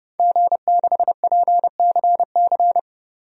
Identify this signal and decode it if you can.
Morse code, heard as a keyed tone, about 32 wpm, 700 Hz. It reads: G6PCC
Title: DXCC-100 Call Signs - 5 Characters - 30wpm